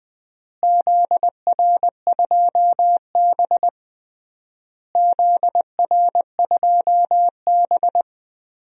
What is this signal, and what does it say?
Morse code, heard as a keyed tone, about 20 wpm, 700 Hz. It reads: ZR2B ZR2B
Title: DXCC-100 Call Signs - 4 Characters - Repeated 3x - 20wpm